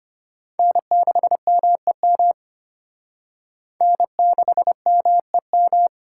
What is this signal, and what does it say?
Morse code, heard as a keyed tone, about 28 wpm, 700 Hz. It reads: N6MEM N6MEM
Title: Speed Racing - USA 1x3 Call Signs 25wpm